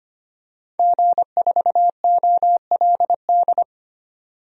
Morse code, heard as a keyed tone, about 25 wpm, 700 Hz. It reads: G4OLD
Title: DXCC-100 Call Signs - 5 Characters - 25wpm